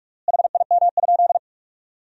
Morse code, heard as a keyed tone, about 41 wpm, 700 Hz. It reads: HIM?
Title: 3x Word-Spacing - Sentences from Top 400 Words - 45wpm